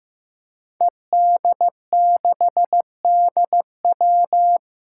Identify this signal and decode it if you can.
Morse code, heard as a keyed tone, about 15 wpm, 700 Hz. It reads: ED6DW
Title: Five Challenging Letter-Number Combinations 15wpm